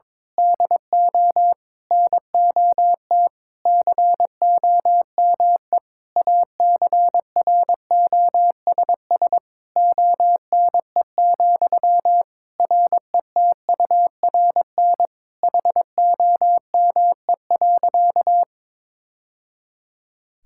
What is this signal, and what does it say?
Morse code, heard as a keyed tone, about 22 wpm, 700 Hz. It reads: DO NOT COME ACROSS ONE, RETURN HOME.
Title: Sentences from Top 1000 Words - Review Part 5 - 22wpm